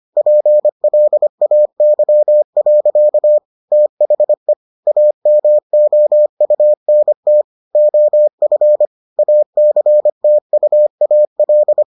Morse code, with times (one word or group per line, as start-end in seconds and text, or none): 0.2-3.4 PLAY.
3.7-4.5 THE
4.9-7.4 AMOUNT
7.7-8.9 OF
9.2-11.8 ACTUAL